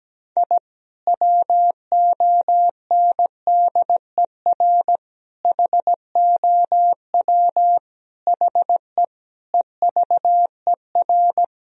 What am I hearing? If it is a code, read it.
Morse code, heard as a keyed tone, about 17 wpm, 700 Hz. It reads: I WONDER HOW HE EVER
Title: Sentences from Top 1000 Words - Review Part 3 - 17wpm